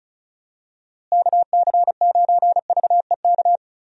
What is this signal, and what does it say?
Morse code, heard as a keyed tone, about 34 wpm, 700 Hz. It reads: KC9VEK
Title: USA Call Signs - 6 Characters - Repeated 3x - 35wpm